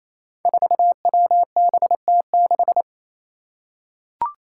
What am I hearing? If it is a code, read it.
Morse code, heard as a keyed tone, about 27 wpm, 700 Hz. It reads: 4WBT6
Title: Five Challenging Letter-Number Combinations 28wpm